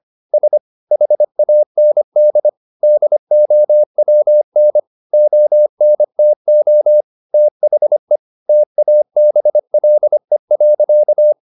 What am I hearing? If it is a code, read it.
Morse code, heard as a keyed tone, about 25 wpm, 600 Hz. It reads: S HAND DOWN ONTO THE TABLE.
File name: the old man and the sea part 1 25wpm